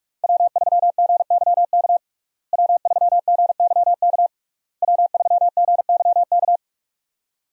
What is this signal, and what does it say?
Morse code, heard as a keyed tone, about 41 wpm, 700 Hz. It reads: W3GYK W3GYK W3GYK
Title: USA Call Signs - 5 Characters - Repeated 3x - 45wpm